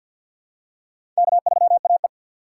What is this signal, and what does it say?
Morse code, heard as a keyed tone, about 49 wpm, 700 Hz. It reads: K3AE
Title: USA Call Signs - 6x Character Spacing - 50wpm